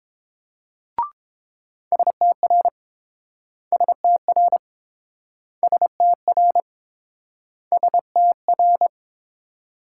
Morse code, heard as a keyed tone, about 29 wpm, 700 Hz. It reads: STR STR STR STR